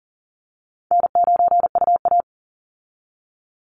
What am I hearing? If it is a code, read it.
Morse code, heard as a keyed tone, about 40 wpm, 700 Hz. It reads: N9UA
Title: Easy Call Signs - 40wpm